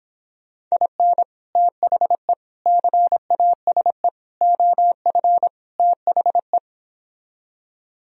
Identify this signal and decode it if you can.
Morse code, heard as a keyed tone, about 26 wpm, 700 Hz. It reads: IN THE CASE OF THE